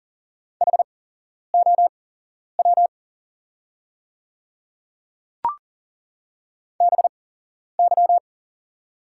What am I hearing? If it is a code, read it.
Morse code, heard as a keyed tone, about 40 wpm, spaced as for 11 wpm, 690 Hz. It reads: HOW BY
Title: 8x Character Spacing - Top 300 Words - 40wpm